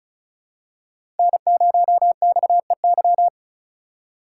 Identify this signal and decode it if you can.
Morse code, heard as a keyed tone, about 36 wpm, 700 Hz. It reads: N0XEY
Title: USA Call Signs - 5 Characters - 2x Character Spacing - 35wpm